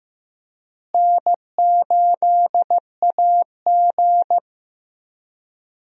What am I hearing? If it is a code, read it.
Morse code, heard as a keyed tone, about 15 wpm, 700 Hz. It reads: N8AG
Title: USA Call Signs - 4 Characters - Repeated 2x - 15wpm